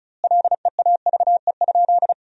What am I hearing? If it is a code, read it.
Morse code, heard as a keyed tone, about 36 wpm, 700 Hz. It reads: LEAVE?
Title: 2x Word-Spacing - Sentences from Top 400 Words - 35wpm